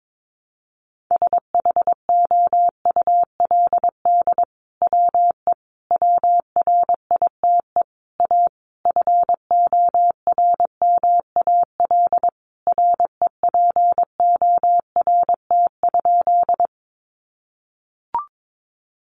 Morse code, heard as a keyed tone, about 22 wpm, 700 Hz. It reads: SHOULD WE WRITE A FORMAL REPORT?
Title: Sentences from Top 1300 Words 22wpm